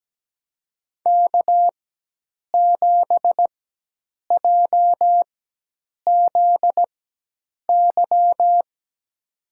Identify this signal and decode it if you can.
Morse code, heard as a keyed tone, about 17 wpm, spaced as for 8 wpm, 700 Hz. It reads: K7JZY
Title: USA Call Signs - 5 Characters - 4x Character Spacing - 17wpm